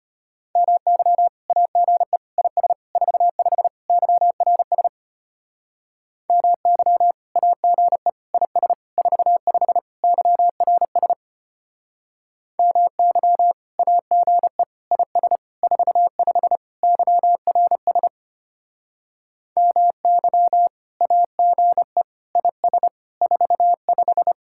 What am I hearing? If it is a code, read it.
Morse code, heard as a keyed tone, about 32 wpm, 700 Hz. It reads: MY AGE IS 45 YRS MY AGE IS 45 YRS MY AGE IS 45 YRS MY AGE IS 45